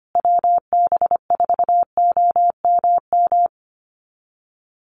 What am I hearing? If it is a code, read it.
Morse code, heard as a keyed tone, about 25 wpm, 700 Hz. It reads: WB4OMM
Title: Call Signs Repeated 2x - 25wpm